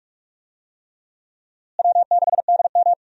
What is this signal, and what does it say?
Morse code, heard as a keyed tone, about 47 wpm, 700 Hz. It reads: W6DM
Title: USA Call Signs - 4 Characters - Repeated 3x - 45wpm